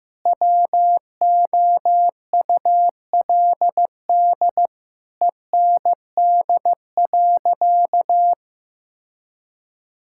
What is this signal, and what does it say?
Morse code, heard as a keyed tone, about 15 wpm, 700 Hz. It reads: WOULD END.
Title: Sentences from Top 1600 Words 15wpm